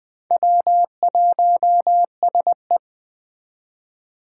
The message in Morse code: W1SE

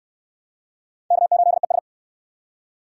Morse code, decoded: K8ES

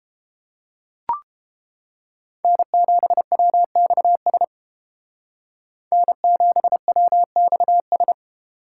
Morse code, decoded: N7WXS N7WXS

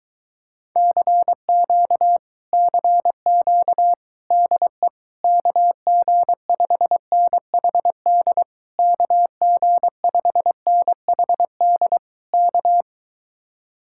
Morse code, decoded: CQ CQ DE KG5NHD KG5NHD K

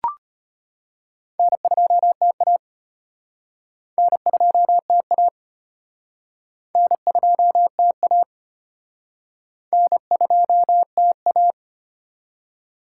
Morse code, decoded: N2TA N2TA N2TA N2TA

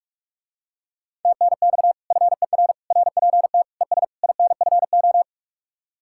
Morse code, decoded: TNX FER RPT ES INFO

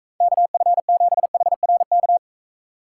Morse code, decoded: KF7HRK

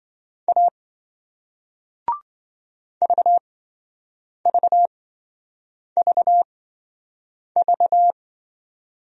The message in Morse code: A VVVV

